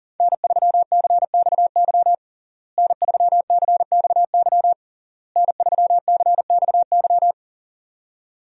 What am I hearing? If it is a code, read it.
Morse code, heard as a keyed tone, about 40 wpm, 700 Hz. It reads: N3CXY N3CXY N3CXY
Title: USA Call Signs - 5 Characters - Repeated 3x - 40wpm